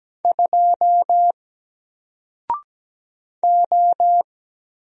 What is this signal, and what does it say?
Morse code, heard as a keyed tone, about 17 wpm, 700 Hz. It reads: 2 O
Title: Single Character (Letter:Number:Punctuation:Prosign) 17wpm